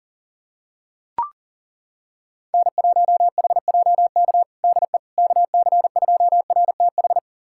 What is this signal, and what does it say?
Morse code, heard as a keyed tone, about 37 wpm, 700 Hz. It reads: N1HJK DE KC2RTH
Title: USA Callsign DE Callsign 40wpm